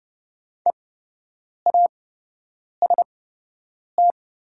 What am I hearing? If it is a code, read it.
Morse code, heard as a keyed tone, about 30 wpm, spaced as for 8 wpm, 700 Hz. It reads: EAST
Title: 8x Character Spacing - Top 500 Words - 30wpm